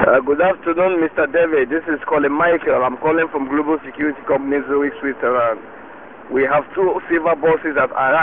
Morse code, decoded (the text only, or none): none